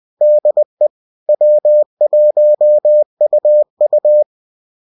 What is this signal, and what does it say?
Morse code, heard as a keyed tone, about 20 wpm, 600 Hz. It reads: DE W1UU